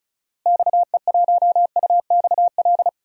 Morse code, heard as a keyed tone, about 36 wpm, 700 Hz. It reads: XE1UXL
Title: DXCC-100 Call Signs - 6 Characters - Repeated 2x - 35wpm